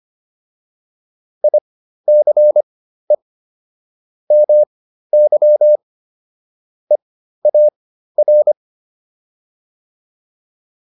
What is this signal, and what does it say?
Morse code, heard as a keyed tone, about 25 wpm, spaced as for 13 wpm, 600 Hz. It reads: ICE MY EAR